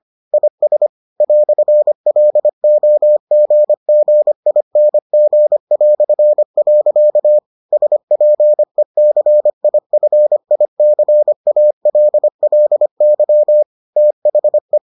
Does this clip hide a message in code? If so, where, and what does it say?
Yes: Morse code from 0.3 to 14.8 s, heard as a keyed tone, about 25 wpm, 600 Hz. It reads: IS "LOGGING". SPECIFICALLY THE